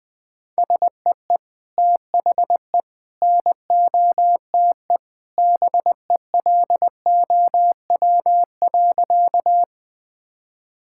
Morse code, heard as a keyed tone, about 20 wpm, 700 Hz. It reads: SEE THE NOTE BELOW.
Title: Sentences from Top 1000 Words - Review Part 5 - 20wpm